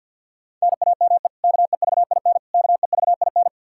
Morse code, heard as a keyed tone, about 51 wpm, 700 Hz. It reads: NAME KEVIN KEVIN
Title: Practice QSO - Exchange 1 - Beginning - 50wpm